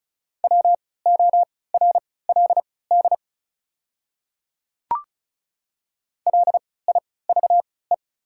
Morse code, heard as a keyed tone, about 36 wpm, spaced as for 20 wpm, 690 Hz. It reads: WORLD LIVE
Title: 3x Character Spacing - Top 500 Words - 35wpm